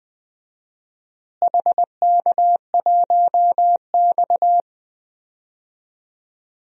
Morse code, heard as a keyed tone, about 20 wpm, 700 Hz. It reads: HK1X
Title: DXCC-100 Call Signs - 4 Characters - Repeated 2x - 20wpm